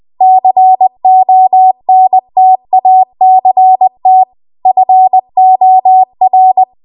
Morse code, heard as a keyed tone, about 20 wpm, 750 Hz. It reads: CONTACT FOR
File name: Archive / 20 WPM / 20WPM